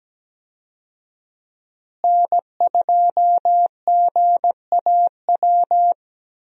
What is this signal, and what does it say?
Morse code, heard as a keyed tone, about 17 wpm, 700 Hz. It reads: N2GAW
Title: USA Call Signs - 5 Characters - Repeated 3x - 17wpm